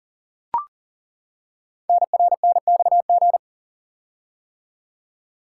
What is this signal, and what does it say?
Morse code, heard as a keyed tone, about 38 wpm, 700 Hz. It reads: NRNXG
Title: Code Groups - 5 Letters - 40wpm